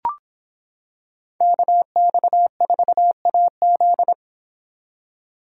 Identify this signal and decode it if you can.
Morse code, heard as a keyed tone, about 25 wpm, 700 Hz. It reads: KX4AZ